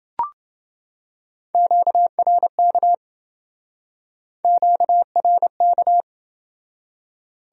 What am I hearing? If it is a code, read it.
Morse code, heard as a keyed tone, about 28 wpm, 700 Hz. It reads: QRK QRK